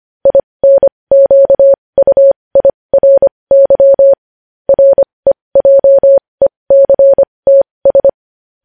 Morse code, heard as a keyed tone, about 25 wpm, spaced as for 20 wpm, 550 Hz. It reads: INQUIRY REJECTS
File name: podcast / News headlines-1x-25x20